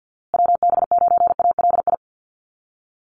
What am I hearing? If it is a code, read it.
Morse code, heard as a keyed tone, about 50 wpm, 700 Hz. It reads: WB9ALI
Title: USA Call Signs Repeated 3x 50wpm